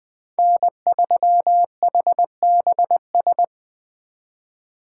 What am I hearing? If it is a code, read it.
Morse code, heard as a keyed tone, about 23 wpm, 700 Hz. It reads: N3HBS
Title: Speed Racing - USA 1x3 Call Signs 20wpm